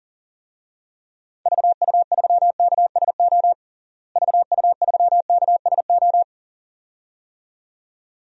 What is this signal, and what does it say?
Morse code, heard as a keyed tone, about 40 wpm, 700 Hz. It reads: VU3KSO VU3KSO
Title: DXCC-100 Call Signs - 6 Characters - Repeated 2x - 40wpm